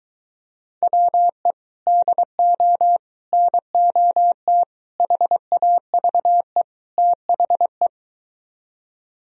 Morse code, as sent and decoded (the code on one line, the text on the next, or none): .-- . / -.. --- / -. --- - / .... .- ...- . / - .... .
WE DO NOT HAVE THE